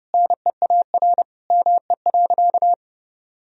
Morse code, heard as a keyed tone, about 30 wpm, 700 Hz. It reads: NEAR ME.
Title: Sentences from Top 1000 Words - Review Part 1 - 30wpm